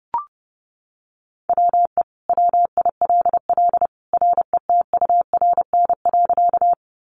Morse code, heard as a keyed tone, about 30 wpm, 700 Hz. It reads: WE WILL RETURN.
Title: Sentences from Top 800 Words 30wpm